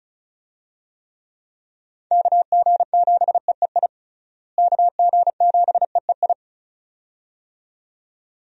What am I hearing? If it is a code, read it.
Morse code, heard as a keyed tone, about 34 wpm, 700 Hz. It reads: KG7EEI KG7EEI
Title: USA Call Signs - 6 Characters - Repeated 2x - 35wpm